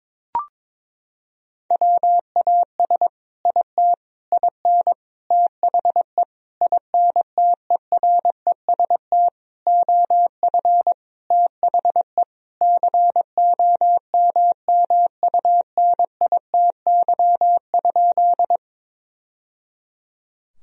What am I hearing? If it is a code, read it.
Morse code, heard as a keyed tone, about 22 wpm, 700 Hz. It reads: WAS IT IN THE INTEREST OF THE COMMUNITY?
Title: Sentences from Top 1000 Words - Review Part 4 - 22wpm